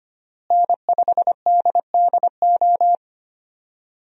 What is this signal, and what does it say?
Morse code, heard as a keyed tone, about 25 wpm, 700 Hz. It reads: N5DDO
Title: USA Call Signs - 5 Characters - Repeated 3x - 25wpm